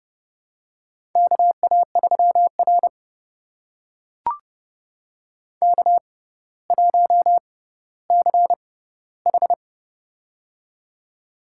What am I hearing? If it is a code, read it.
Morse code, heard as a keyed tone, about 30 wpm, spaced as for 15 wpm, 700 Hz. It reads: KA3R K1CH
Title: USA Call Signs - 6x Character Spacing - 30wpm